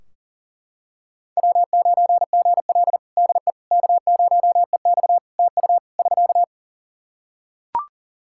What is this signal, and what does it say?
Morse code, heard as a keyed tone, about 37 wpm, 700 Hz. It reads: W9GL DE K0EX TU <SK>
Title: Practice QSO - Exchange 4 - Ending - 40wpm